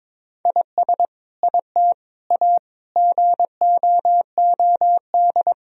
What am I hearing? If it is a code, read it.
Morse code, heard as a keyed tone, about 22 wpm, 700 Hz. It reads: IS IT A GOOD
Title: Sentences from Top 1000 Words - Review Part 3 - 22wpm